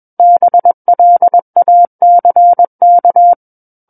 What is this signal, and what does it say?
Morse code, heard as a keyed tone, about 21 wpm, 700 Hz. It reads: BLACK
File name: Book One / 12-Chapter 12 - 21 WPM